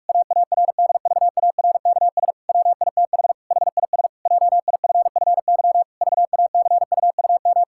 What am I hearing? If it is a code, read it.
Morse code, heard as a keyed tone, about 44 wpm, 700 Hz. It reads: AARDVARKS WITH HIS JIFFY VACUUM
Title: Pangram Challenge v2 - 45wpm